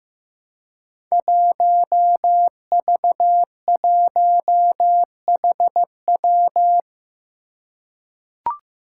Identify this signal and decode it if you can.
Morse code, heard as a keyed tone, about 15 wpm, 700 Hz. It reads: 1V1HW
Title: Five Challenging Letter-Number Combinations 15wpm